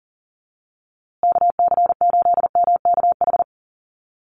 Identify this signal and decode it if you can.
Morse code, heard as a keyed tone, about 40 wpm, 700 Hz. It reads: KC8MKH
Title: USA Call Signs Repeated 2x 40wpm